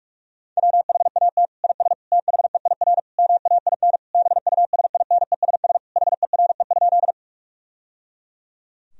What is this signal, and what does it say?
Morse code, heard as a keyed tone, about 46 wpm, 700 Hz. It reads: WHAT IS THEIR MAIN BUSINESS HERE?
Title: Sentences from Top 1000 Words - Review Part 2 - 45wpm